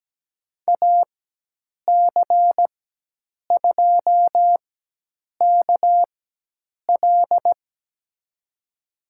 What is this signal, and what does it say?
Morse code, heard as a keyed tone, about 17 wpm, spaced as for 8 wpm, 700 Hz. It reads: AC2KL
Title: USA Call Signs - 5 Characters - 4x Character Spacing - 17wpm